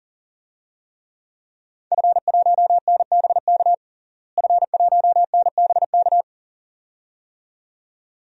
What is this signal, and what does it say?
Morse code, heard as a keyed tone, about 40 wpm, 700 Hz. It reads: F1NBK F1NBK
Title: DXCC-100 Call Signs - 5 Characters - Repeated 2x - 40wpm